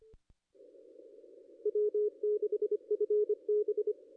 Morse code, heard as a keyed tone, about 28 wpm, 440 Hz. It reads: W6FB
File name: contests / ss-cw-k5tr-op-n5rz -0400